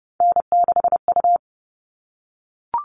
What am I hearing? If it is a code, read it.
Morse code, heard as a keyed tone, about 30 wpm, 700 Hz. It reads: N6U
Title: All Cut Numbers - 3-Digit 30wpm